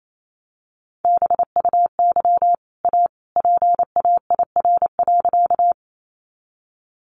Morse code, heard as a keyed tone, about 28 wpm, 700 Hz. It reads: BUY A PAIR.
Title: Sentences from Top 1200 Words 28wpm